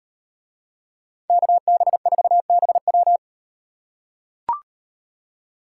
Morse code, heard as a keyed tone, about 38 wpm, 700 Hz. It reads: KB4BW